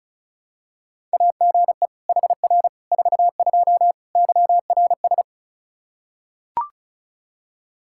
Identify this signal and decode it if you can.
Morse code, heard as a keyed tone, about 36 wpm, 700 Hz. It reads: AGE HR 42 YRS
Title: Practice QSO - Exchange 3 - More Info - 35wpm